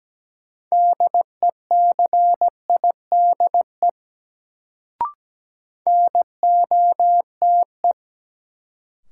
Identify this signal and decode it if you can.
Morse code, heard as a keyed tone, about 17 wpm, 700 Hz. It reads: DECIDE NOTE
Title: Sentences from Top 500 Words - New Word Drill - 17wpm